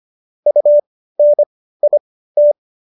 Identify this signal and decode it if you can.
Morse code, heard as a keyed tone, about 25 wpm, spaced as for 15 wpm, 590 Hz. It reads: UNIT